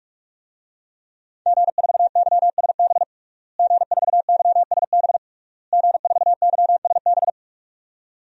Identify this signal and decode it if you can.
Morse code, heard as a keyed tone, about 46 wpm, 700 Hz. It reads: G4YSB G4YSB G4YSB
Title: DXCC-100 Call Signs - 5 Characters - Repeated 3x - 45wpm